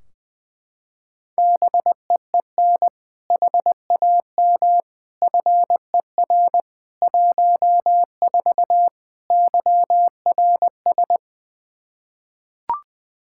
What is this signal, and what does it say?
Morse code, heard as a keyed tone, about 20 wpm, 700 Hz. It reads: BEEN HAM FER 14 YRS
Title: Practice QSO - Exchange 3 - More Info - 20wpm